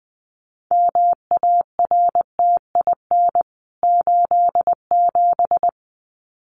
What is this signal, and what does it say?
Morse code, heard as a keyed tone, about 20 wpm, 700 Hz. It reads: MARTIN 87
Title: CWT Exchange 20wpm